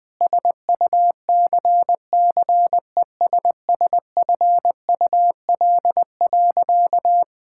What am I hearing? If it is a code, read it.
Morse code, heard as a keyed tone, about 20 wpm, 700 Hz. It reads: SUCCESSFUL.
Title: Sentences from Top 1000 Words - Review Part 5 - 20wpm